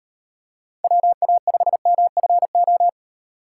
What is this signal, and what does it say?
Morse code, heard as a keyed tone, about 38 wpm, 700 Hz. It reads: WA5MFO